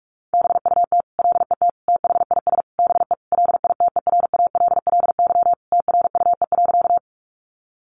Morse code, heard as a keyed tone, about 44 wpm, 700 Hz. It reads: BUT LET THIS BE LITERALLY TRUE.
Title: Sentences from Top 1400 Words 45wpm